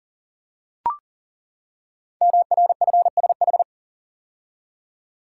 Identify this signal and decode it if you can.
Morse code, heard as a keyed tone, about 41 wpm, 700 Hz. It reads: MRFSH
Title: Code Groups - 5 Letters - 40wpm